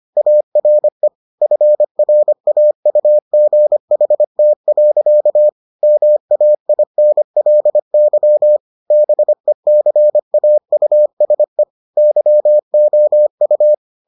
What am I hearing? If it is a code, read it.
Morse code, heard as a keyed tone, about 25 wpm, 600 Hz. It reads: ARE FRAUGHT. MAINLY BECAUSE YOU